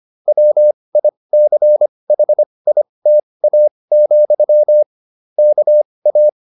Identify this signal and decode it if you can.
Morse code, heard as a keyed tone, about 25 wpm, spaced as for 20 wpm, 600 Hz. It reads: WICHITA, KA
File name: QSO309 20